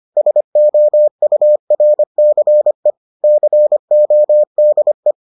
Morse code, heard as a keyed tone, about 25 wpm, 600 Hz. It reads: SOURCE CODE